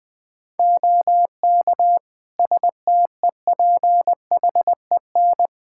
Morse code, heard as a keyed tone, about 20 wpm, 700 Hz. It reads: OK STEPHEN
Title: Practice QSO - Exchange 4 - Ending - 20wpm